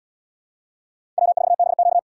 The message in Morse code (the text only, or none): W4BQ